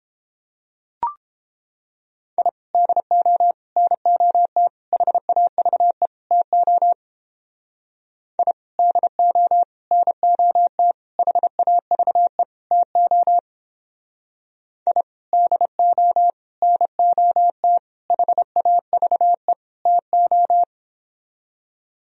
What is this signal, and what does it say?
Morse code, heard as a keyed tone, about 29 wpm, 700 Hz. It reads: I DO NOT HAVE TO I DO NOT HAVE TO I DO NOT HAVE TO